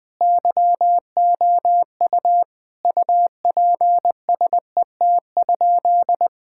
Morse code, heard as a keyed tone, about 20 wpm, 700 Hz. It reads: YOU UPSET?
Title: Sentences from Top 1000 Words - Review Part 5 - 20wpm